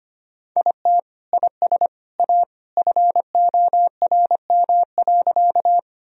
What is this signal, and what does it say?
Morse code, heard as a keyed tone, about 25 wpm, 700 Hz. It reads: IT IS A FORM.
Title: Sentences from Top 1000 Words - Review Part 1 - 25wpm